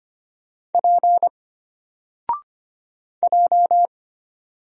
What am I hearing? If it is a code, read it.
Morse code, heard as a keyed tone, about 25 wpm, 700 Hz. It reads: P J
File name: Single-Letter-Number-25wpm